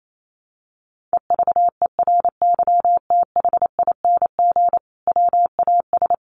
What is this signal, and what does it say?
Morse code, heard as a keyed tone, about 28 wpm, 700 Hz. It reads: EVERYTHING WAS
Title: Sentences from Top 1000 Words 28wpm